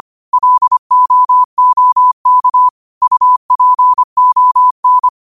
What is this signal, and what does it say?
Morse code, heard as a keyed tone, about 25 wpm, 1000 Hz. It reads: LOOK UPON
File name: qotdfiles / 25wpm